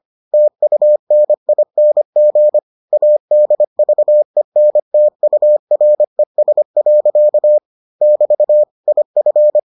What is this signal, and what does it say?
Morse code, heard as a keyed tone, about 25 wpm, 600 Hz. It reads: TUNING ADVENTURES. <BT> IF